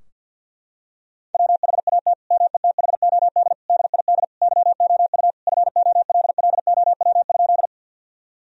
Morse code, heard as a keyed tone, about 48 wpm, 700 Hz. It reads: WHAT METHOD DID YOU FOLLOW?
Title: Sentences from Top 1000 Words - Review Part 3 - 50wpm